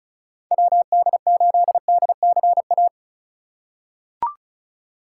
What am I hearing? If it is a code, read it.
Morse code, heard as a keyed tone, about 36 wpm, 700 Hz. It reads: WD8DCA
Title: USA Call Signs - 3x Character Spacing - 35wpm